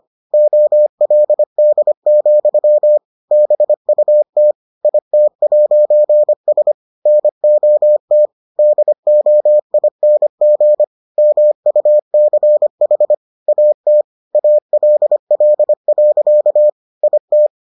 Morse code, heard as a keyed tone, about 25 wpm, 600 Hz. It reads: OLD, BUT IT'S NOT DOING MUCH AT ALL. IT